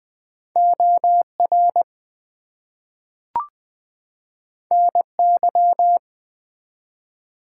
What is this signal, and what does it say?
Morse code, heard as a keyed tone, about 20 wpm, 700 Hz. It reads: OR NY
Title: US States and Canadian Provinces 20wpm